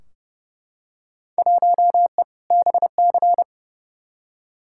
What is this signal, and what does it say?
Morse code, heard as a keyed tone, about 30 wpm, 700 Hz. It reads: 1E BC
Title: Field Day Exchanges - 30wpm